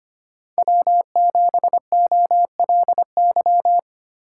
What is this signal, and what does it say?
Morse code, heard as a keyed tone, about 25 wpm, 700 Hz. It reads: W7OLY
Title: USA Call Signs - 5 Characters - 4x Character Spacing - 25wpm